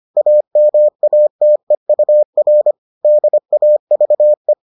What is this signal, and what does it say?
Morse code, heard as a keyed tone, about 25 wpm, 600 Hz. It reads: AMATEUR DAVE